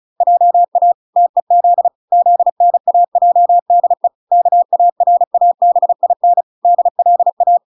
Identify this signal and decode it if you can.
Morse code, heard as a keyed tone, about 34 wpm, 700 Hz. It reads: JA TEZ ZNAJDE KARABIN DLA